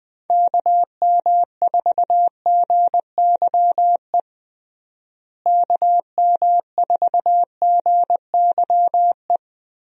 Morse code, heard as a keyed tone, about 20 wpm, 700 Hz. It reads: KM4GYE KM4GYE
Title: USA Call Signs - 6 Characters - Repeated 3x - 20wpm